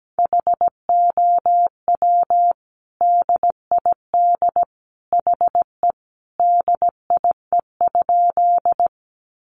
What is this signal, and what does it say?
Morse code, heard as a keyed tone, about 17 wpm, 700 Hz. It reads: HOW DID HE DIE?
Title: Sentences from Top 1300 Words 17wpm